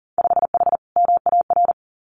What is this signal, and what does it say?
Morse code, heard as a keyed tone, about 40 wpm, 700 Hz. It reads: 5H MAR
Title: Winter Field Day Exchanges Encore 40wpm